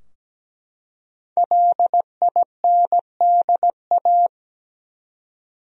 Morse code, heard as a keyed tone, about 17 wpm, 700 Hz. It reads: LINDA
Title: USA Common Names - Max 5 Letters - 17wpm